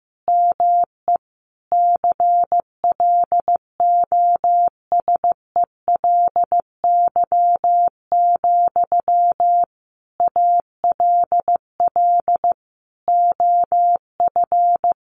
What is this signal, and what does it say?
Morse code, heard as a keyed tone, about 14 wpm, 700 Hz. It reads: ME CLOSELY, ALL OF
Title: Sentences from Top 1400 Words 15wpm